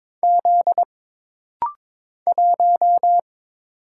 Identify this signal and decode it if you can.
Morse code, heard as a keyed tone, about 22 wpm, 700 Hz. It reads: Z 1
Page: This rapid-fire timing is geared to push your ICR (Instant Character Recognition) skills.